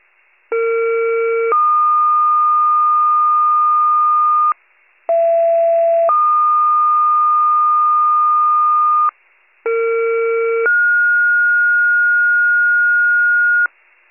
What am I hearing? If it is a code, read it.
Two-tone page, heard as a one-second tone, then a three-second tone, about 460 then 1220 Hz, about 670 then 1220 Hz, about 460 then 1430 Hz. It is heard from 0.5 to 13.7 s.
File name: pager / 21 FIRE / wcalert